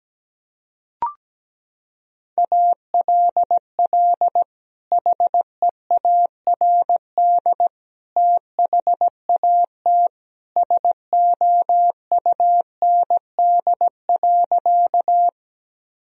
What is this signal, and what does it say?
Morse code, heard as a keyed tone, about 17 wpm, 700 Hz. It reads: ALL HEARD THAT SOUND.
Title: Sentences from Top 1000 Words - Review Part 1 - 17wpm